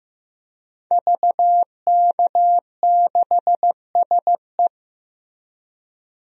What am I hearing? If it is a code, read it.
Morse code, heard as a keyed tone, about 15 wpm, 700 Hz. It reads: VK6SE
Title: DXCC-100 Call Signs - 5 Characters - 15wpm